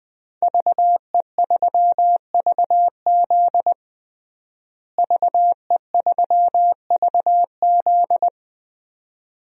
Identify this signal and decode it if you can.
Morse code, heard as a keyed tone, about 21 wpm, 700 Hz. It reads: VE3VZ VE3VZ
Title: DXCC-100 Call Signs - 5 Characters - Repeated 2x - 20wpm